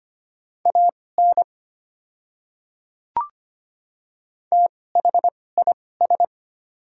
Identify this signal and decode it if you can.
Morse code, heard as a keyed tone, about 25 wpm, spaced as for 18 wpm, 700 Hz. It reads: AN THIS
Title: 2x Character Spacing - Top 100 Words - 25wpm